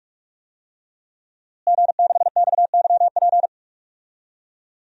Morse code, heard as a keyed tone, about 41 wpm, 700 Hz. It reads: G6XYP
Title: DXCC-100 Call Signs - 5 Characters - Repeated 3x - 45wpm